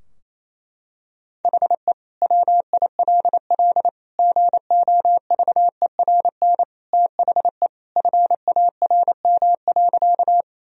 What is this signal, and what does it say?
Morse code, heard as a keyed tone, about 28 wpm, 700 Hz. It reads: HE WILL GOVERN THE FARM.
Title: Sentences from Top 1000 Words - Review Part 2 - 28wpm